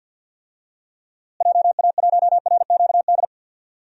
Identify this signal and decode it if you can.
Morse code, heard as a keyed tone, about 46 wpm, 700 Hz. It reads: JA1RQD
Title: DXCC-100 Call Signs - 6 Characters - Repeated 3x - 50wpm